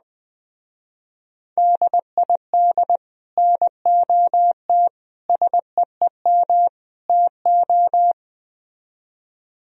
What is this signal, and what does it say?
Morse code, heard as a keyed tone, about 20 wpm, 700 Hz. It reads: DID NOT SEEM TO